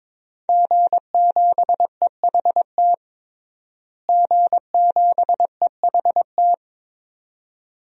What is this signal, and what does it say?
Morse code, heard as a keyed tone, about 22 wpm, 700 Hz. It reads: G7EHT G7EHT
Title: DXCC-100 Call Signs - 5 Characters - Repeated 2x - 22wpm